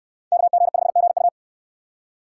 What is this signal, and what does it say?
Morse code, heard as a keyed tone, about 68 wpm, 700 Hz. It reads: KG5GU